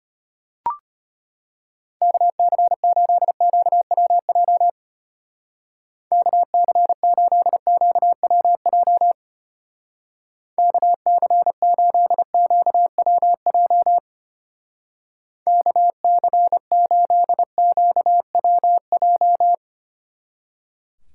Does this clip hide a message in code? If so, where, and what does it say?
Yes: Morse code from 2.0 to 19.5 s, heard as a keyed tone, about 32 wpm, 700 Hz. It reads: KC8QWJ KC8QWJ KC8QWJ KC8QWJ